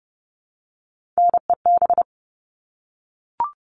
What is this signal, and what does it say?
Morse code, heard as a keyed tone, about 30 wpm, 700 Hz. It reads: NEB
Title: All Cut Numbers - 3-Digit 30wpm